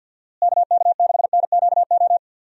Morse code, heard as a keyed tone, about 52 wpm, 700 Hz. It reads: KK6NQO
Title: USA Call Signs - 6 Characters - Repeated 2x - 50wpm